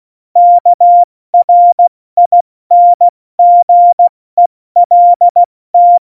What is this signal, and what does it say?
Morse code, heard as a keyed tone, about 16 wpm, 700 Hz. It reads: KRINGELT